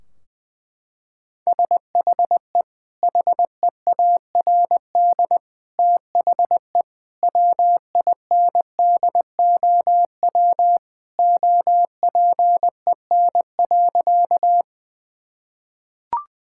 Morse code, heard as a keyed tone, about 21 wpm, 700 Hz. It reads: SHE HEARD THE WINDOW OPEN.
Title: Sentences from Top 1600 Words 20wpm